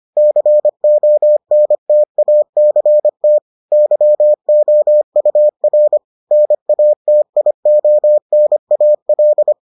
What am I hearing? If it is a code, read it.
Morse code, heard as a keyed tone, about 25 wpm, 600 Hz. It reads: CONTACT YOUR NATIONAL